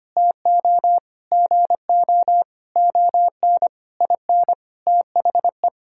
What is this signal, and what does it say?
Morse code, heard as a keyed tone, about 25 wpm, 700 Hz. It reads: TO GO ON IN THE
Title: Sentences from Top 1000 Words - Review Part 5 - 25wpm